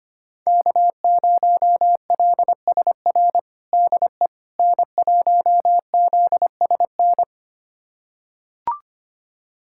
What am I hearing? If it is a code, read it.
Morse code, heard as a keyed tone, about 25 wpm, 700 Hz. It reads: K0LSR DE N1ZSN
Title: USA Callsign DE Callsign - 5 Characters - 25wpm